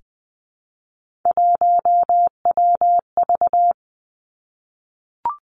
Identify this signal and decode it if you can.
Morse code, heard as a keyed tone, about 20 wpm, 700 Hz. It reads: 1WV
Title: Three Challenging Letter-Number Combinations 20wpm